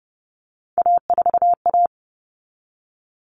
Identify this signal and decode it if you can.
Morse code, heard as a keyed tone, about 30 wpm, 700 Hz. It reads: A4A
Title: All Cut Numbers - 3-Digit 30wpm